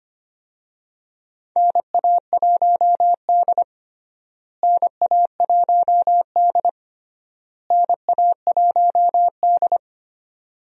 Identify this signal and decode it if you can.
Morse code, heard as a keyed tone, about 25 wpm, 700 Hz. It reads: NA1D NA1D NA1D
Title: USA Call Signs - 4 Characters - Repeated 3x - 25wpm